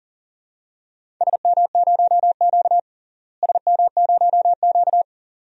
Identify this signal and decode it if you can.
Morse code, heard as a keyed tone, about 38 wpm, 700 Hz. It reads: SM0Q SM0Q
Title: DXCC-100 Call Signs - 4 Characters - Repeated 2x - 40wpm